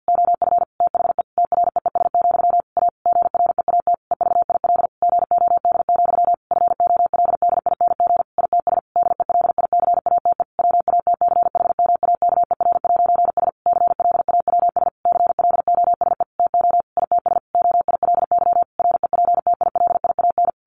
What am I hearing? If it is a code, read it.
Morse code, heard as a keyed tone, about 48 wpm, 700 Hz. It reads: OF THE TREES, A GREAT EVIL GOD, FOLDING ITS DELICATE WATCHMAKER'S CLAWS CLOSE TO ITS OILY REPTILIAN
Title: A Sound Of Thunder 50wpm